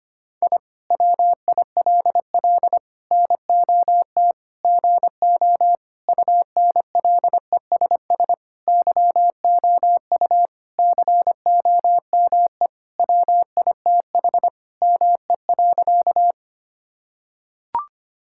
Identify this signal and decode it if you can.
Morse code, heard as a keyed tone, about 25 wpm, 700 Hz. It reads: I WILL NOT GO UNLESS YOU COME WITH ME.
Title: Sentences from Top 1000 Words - Review Part 6 - 25wpm